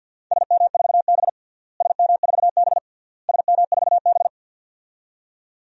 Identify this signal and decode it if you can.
Morse code, heard as a keyed tone, about 50 wpm, 700 Hz. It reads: SM4B SM4B SM4B
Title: DXCC-100 Call Signs - 4 Characters - Repeated 3x - 50wpm